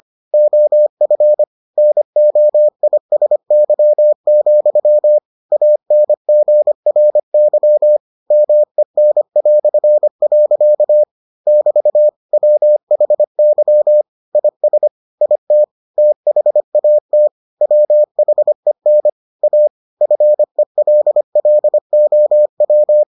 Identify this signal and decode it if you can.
Morse code, heard as a keyed tone, about 25 wpm, 600 Hz. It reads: OF NOISY, ANGRY MEN". <BT> WHY IS IT THAT WHEN A FELLOW